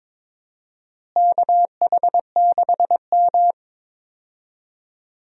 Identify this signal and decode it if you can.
Morse code, heard as a keyed tone, about 21 wpm, 700 Hz. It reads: KH6M
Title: DXCC-100 Call Signs - 4 Characters - Repeated 2x - 22wpm